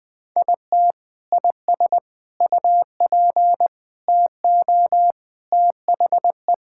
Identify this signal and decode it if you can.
Morse code, heard as a keyed tone, about 21 wpm, 700 Hz. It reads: IT IS UP TO THE
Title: Sentences with 3-Word Phrases - 20wpm